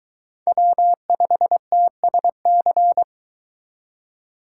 Morse code, heard as a keyed tone, about 22 wpm, 700 Hz. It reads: W5TSC